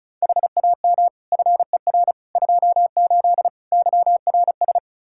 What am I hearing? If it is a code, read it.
Morse code, heard as a keyed tone, about 36 wpm, 700 Hz. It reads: HAM FER 28 YRS
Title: Practice QSO - Exchange 3 - More Info - 35wpm